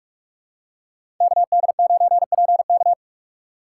Morse code, heard as a keyed tone, about 46 wpm, 700 Hz. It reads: KD9PK
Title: USA Call Signs - 5 Characters - 4x Character Spacing - 45wpm